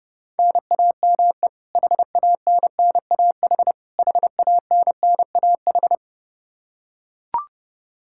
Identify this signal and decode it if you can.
Morse code, heard as a keyed tone, about 29 wpm, 700 Hz. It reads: NAME HANNAH HANNAH
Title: Practice QSO - Exchange 1 - Beginning - 30wpm